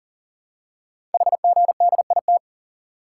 Morse code, heard as a keyed tone, about 42 wpm, 700 Hz. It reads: HGDIT
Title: Code Groups - 5 Letters - 40wpm